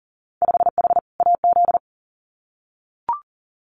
Morse code, heard as a keyed tone, about 38 wpm, 700 Hz. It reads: 5H AZ
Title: Winter Field Day Exchanges Encore 40wpm